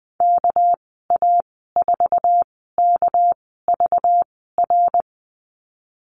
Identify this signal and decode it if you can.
Morse code, heard as a keyed tone, about 20 wpm, spaced as for 15 wpm, 700 Hz. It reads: KA4KVR
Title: 219 Lesson 35g - Number 0 - Call Signs 20wpm